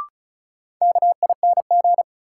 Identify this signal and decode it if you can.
Morse code, heard as a keyed tone, about 36 wpm, 700 Hz. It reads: KING
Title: Sentences from Top 400 Words - New Word Drill - 35wpm